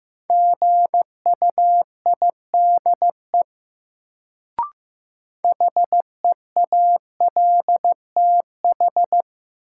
Morse code, heard as a keyed tone, about 15 wpm, 700 Hz. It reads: GUIDE HEALTH
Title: Sentences from Top 600 Words - New Word Drill - 15wpm